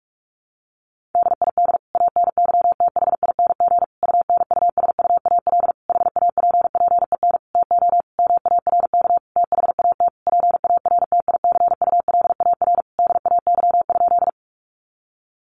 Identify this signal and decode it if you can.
Morse code, heard as a keyed tone, about 46 wpm, 700 Hz. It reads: DID ANYTHING UNUSUAL HAPPEN TO MARK THAT PARTICULAR DAY?
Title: Sentences from Top 1400 Words 45wpm